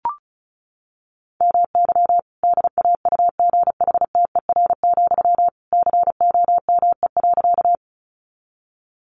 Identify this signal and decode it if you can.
Morse code, heard as a keyed tone, about 34 wpm, 700 Hz. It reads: MY DAUGHTER, COME.
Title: Sentences from Top 1500 Words 35wpm